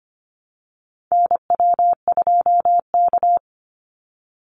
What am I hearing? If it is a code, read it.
Morse code, heard as a keyed tone, about 25 wpm, 700 Hz. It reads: NW2K
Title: Call Signs Repeated 2x - 25wpm